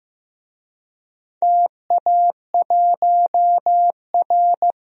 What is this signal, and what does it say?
Morse code, heard as a keyed tone, about 15 wpm, 700 Hz. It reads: TA1R
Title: DXCC-100 Call Signs - 4 Characters - Repeated 2x - 15wpm